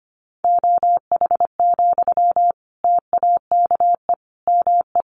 Morse code, heard as a keyed tone, about 25 wpm, 700 Hz. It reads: OH, TAKE ME
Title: Sentences from Top 1400 Words 25wpm